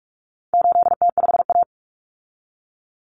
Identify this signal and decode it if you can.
Morse code, heard as a keyed tone, about 44 wpm, 700 Hz. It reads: 8T5A
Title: Common Cut Numbers - 4-Digit 45wpm